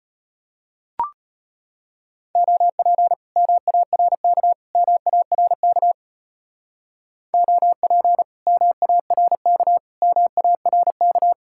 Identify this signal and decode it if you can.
Morse code, heard as a keyed tone, about 33 wpm, 700 Hz. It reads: OP MARK MARK OP MARK MARK